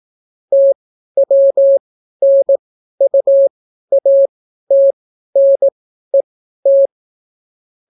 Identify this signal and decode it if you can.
Morse code, heard as a keyed tone, about 18 wpm, spaced as for 12 wpm, 550 Hz. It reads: TWNUATNET